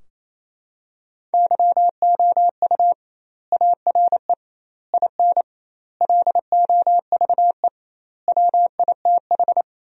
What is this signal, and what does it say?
Morse code, heard as a keyed tone, about 28 wpm, 700 Hz. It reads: YOU ARE IN LOVE WITH
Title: 2x Word-Spacing - Sentences from Top 400 Words - 28wpm